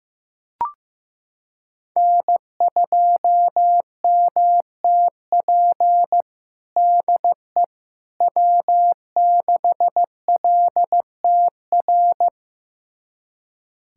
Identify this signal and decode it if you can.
Morse code, heard as a keyed tone, about 15 wpm, 700 Hz. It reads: N2MTP DE W6LTR
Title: USA Callsign DE Callsign - 5 Characters - 15wpm